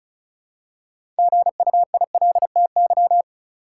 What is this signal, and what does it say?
Morse code, heard as a keyed tone, about 36 wpm, 700 Hz. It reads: GUILTY
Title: Six Letter Words - 35wpm